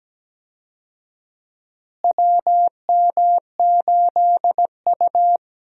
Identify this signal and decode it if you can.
Morse code, heard as a keyed tone, about 17 wpm, 700 Hz. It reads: WM8U
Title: USA Call Signs - 4 Characters - Repeated 3x - 17wpm